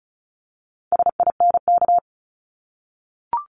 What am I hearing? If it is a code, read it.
Morse code, heard as a keyed tone, about 36 wpm, 700 Hz. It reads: SINK
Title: Multi-letter Phonemes Repeated 3x 35wpm